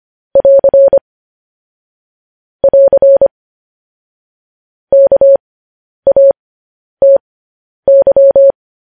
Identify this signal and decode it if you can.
Morse code, heard as a keyed tone, about 25 wpm, spaced as for 10 wpm, 550 Hz. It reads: <AR> <AR> KATY
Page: news headlines in morse code at 10 wpm